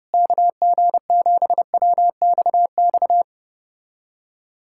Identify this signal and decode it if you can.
Morse code, heard as a keyed tone, about 29 wpm, 700 Hz. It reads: KG7WXX
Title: USA Callsign DE Callsign 30wpm